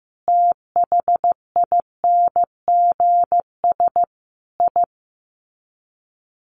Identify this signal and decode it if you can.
Morse code, heard as a keyed tone, about 15 wpm, 700 Hz. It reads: THINGS I
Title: Sets of 5 Words 15wpm